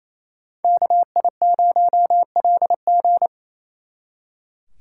Morse code, heard as a keyed tone, about 35 wpm, 700 Hz. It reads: KI0LG
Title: Speed Racing - USA 2x2 Call Signs 28wpm